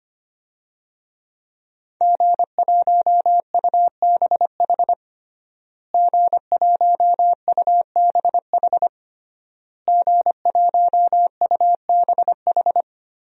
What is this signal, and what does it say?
Morse code, heard as a keyed tone, about 25 wpm, 700 Hz. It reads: G1UBH G1UBH G1UBH
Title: DXCC-100 Call Signs - 5 Characters - Repeated 3x - 25wpm